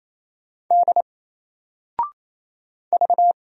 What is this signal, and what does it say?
Morse code, heard as a keyed tone, about 27 wpm, 700 Hz. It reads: D V
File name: Single-Letter-Number-28wpm